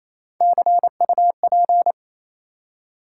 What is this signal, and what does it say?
Morse code, heard as a keyed tone, about 27 wpm, 700 Hz. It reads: CUP
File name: Easy Words - Speed-Racing with Hints - 22wpm